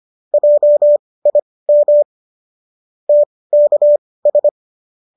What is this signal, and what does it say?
Morse code, heard as a keyed tone, about 25 wpm, spaced as for 18 wpm, 570 Hz. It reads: JIM TKS
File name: qso203 18